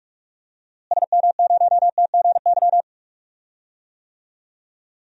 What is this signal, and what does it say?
Morse code, heard as a keyed tone, about 42 wpm, 700 Hz. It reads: SM0TGY
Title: DXCC-100 Call Signs - 6 Characters - Repeated 2x - 45wpm